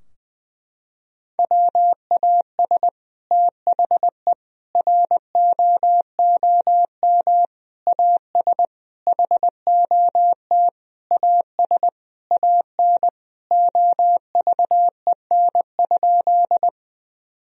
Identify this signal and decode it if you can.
Morse code, heard as a keyed tone, about 21 wpm, 700 Hz. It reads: WAS THE ROOM AS HOT AS AN OVEN?
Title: Sentences from Top 1000 Words - Review Part 4 - 20wpm